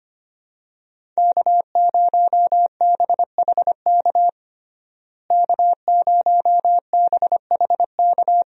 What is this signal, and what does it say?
Morse code, heard as a keyed tone, about 25 wpm, 700 Hz. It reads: K0BHK K0BHK
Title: USA Call Signs - 5 Characters - Repeated 3x - 25wpm